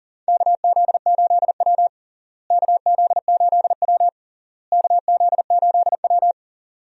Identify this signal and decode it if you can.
Morse code, heard as a keyed tone, about 40 wpm, 700 Hz. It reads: KZ8W KZ8W KZ8W
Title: USA Call Signs - 4 Characters - Repeated 3x - 40wpm